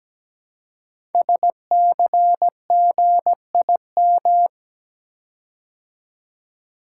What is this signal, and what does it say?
Morse code, heard as a keyed tone, about 17 wpm, 700 Hz. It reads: SCGIM
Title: Code Groups - 5 Letter:Numbers - 17wpm